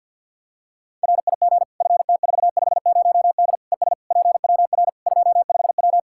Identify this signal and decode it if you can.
Morse code, heard as a keyed tone, about 52 wpm, 700 Hz. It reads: RIG FT450D ES PWR 25W
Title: K1USN SST QSO Practice 50wpm